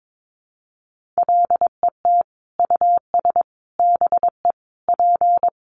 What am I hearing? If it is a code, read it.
Morse code, heard as a keyed tone, about 22 wpm, 700 Hz. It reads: LET US BE P
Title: Sentences from Top 1400 Words 22wpm